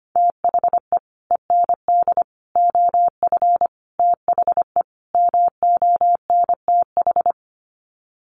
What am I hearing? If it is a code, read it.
Morse code, heard as a keyed tone, about 25 wpm, 700 Hz. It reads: THE END OF THE MONTH
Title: Sets of 5 Words Encore 25wpm